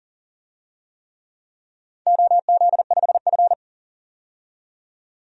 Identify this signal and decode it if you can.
Morse code, heard as a keyed tone, about 40 wpm, 700 Hz. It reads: OZ5F
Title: DXCC-100 Call Signs - 4 Characters - 40wpm